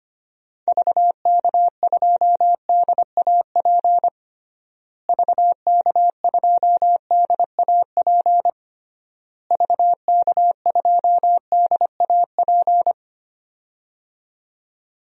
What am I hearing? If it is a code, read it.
Morse code, heard as a keyed tone, about 25 wpm, 700 Hz. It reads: VK2DAP VK2DAP VK2DAP
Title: DXCC-100 Call Signs - 6 Characters - Repeated 3x - 25wpm